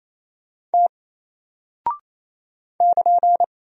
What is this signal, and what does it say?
Morse code, heard as a keyed tone, about 27 wpm, 700 Hz. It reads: T <KN>
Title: Single Character (Letter:Number:Punctuation:Prosign) 28wpm